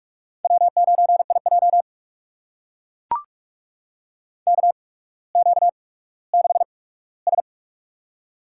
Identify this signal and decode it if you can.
Morse code, heard as a keyed tone, about 44 wpm, spaced as for 19 wpm, 690 Hz. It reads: W9IJ KQ6S
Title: USA Call Signs - 4 Characters - 8x Character Spacing - 45wpm